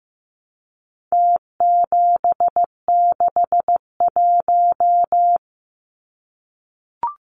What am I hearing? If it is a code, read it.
Morse code, heard as a keyed tone, about 15 wpm, 700 Hz. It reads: T761
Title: Numbers - Contest Style - 4-Digit 15wpm